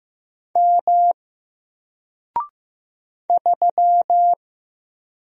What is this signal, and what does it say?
Morse code, heard as a keyed tone, about 15 wpm, 700 Hz. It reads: M 3
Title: Single Character (Letter:Number:Punctuation:Prosign) Alt 15wpm